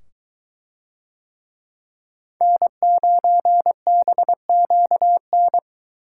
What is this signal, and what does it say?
Morse code, heard as a keyed tone, about 22 wpm, 700 Hz. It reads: N9BQN